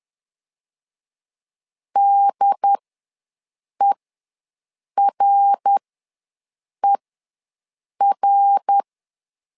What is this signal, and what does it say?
Morse code, heard as a keyed tone, about 10 wpm, spaced as for 6 wpm, 800 Hz. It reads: DERER
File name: LEZIONI / 3 MWRO / 3 MWR ws 10wpm 2151